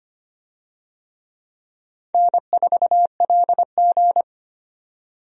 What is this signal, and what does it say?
Morse code, heard as a keyed tone, about 25 wpm, 700 Hz. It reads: N4LG
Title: USA Call Signs - 4 Characters - 25wpm